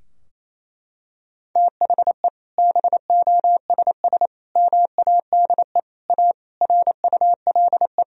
Morse code, heard as a keyed tone, about 28 wpm, 700 Hz. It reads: THE BOSS MADE A RULE
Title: Sentences from Top 1000 Words - Review Part 4 - 28wpm